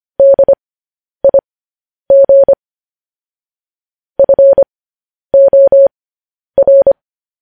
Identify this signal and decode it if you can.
Morse code, heard as a keyed tone, about 25 wpm, spaced as for 10 wpm, 550 Hz. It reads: DIG FOR
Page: news-headlines-in-morse-code-at-10-wpm